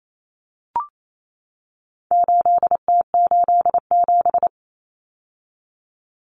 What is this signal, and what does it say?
Morse code, heard as a keyed tone, about 28 wpm, 700 Hz. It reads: 8T87
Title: Common Cut Numbers - 4-Digit 28wpm